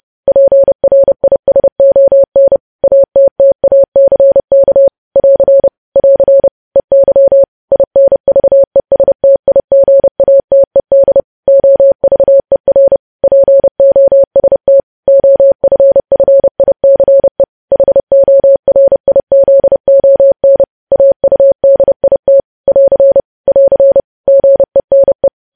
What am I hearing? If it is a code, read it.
Morse code, heard as a keyed tone, about 30 wpm, 550 Hz. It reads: PRISON ATTACK <AR> <AR> EY INVESTIGATED OVER POST OFFICE HORIZON AUDIT <AR> <AR> GENE
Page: news headlines in morse code at 30 wpm